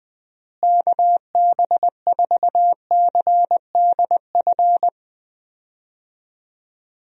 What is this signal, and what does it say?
Morse code, heard as a keyed tone, about 20 wpm, 700 Hz. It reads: KB4CDF
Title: USA Call Signs - 6 Characters - Repeated 2x - 20wpm